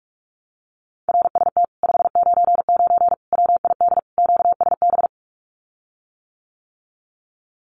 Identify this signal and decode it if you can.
Morse code, heard as a keyed tone, about 44 wpm, 700 Hz. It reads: RST 599 WID QSB
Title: Common QSO Phrases Encore 45wpm